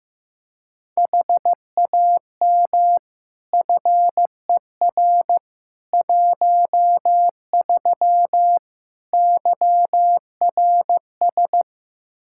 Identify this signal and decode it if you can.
Morse code, heard as a keyed tone, about 15 wpm, 700 Hz. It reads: HAM FER 13 YRS
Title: K1USN SST QSO Practice 15wpm